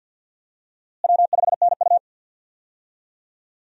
Morse code, heard as a keyed tone, about 53 wpm, 700 Hz. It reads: W5NU
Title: USA Call Signs - 4 Characters - Repeated 2x - 50wpm